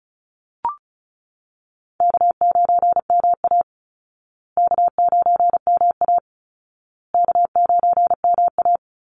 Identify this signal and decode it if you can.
Morse code, heard as a keyed tone, about 34 wpm, 700 Hz. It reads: K9MA K9MA K9MA
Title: Call Signs Repeated 2x - 35wpm